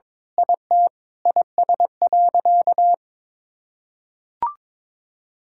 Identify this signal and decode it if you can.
Morse code, heard as a keyed tone, about 22 wpm, 700 Hz. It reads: IT IS.
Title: Sentences from Top 1000 Words - Review Part 6 - 22wpm